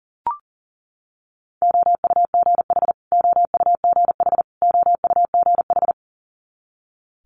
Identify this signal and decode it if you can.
Morse code, heard as a keyed tone, about 40 wpm, 700 Hz. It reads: OUGH OUGH OUGH
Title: Multi-letter Phonemes Repeated 3x 40wpm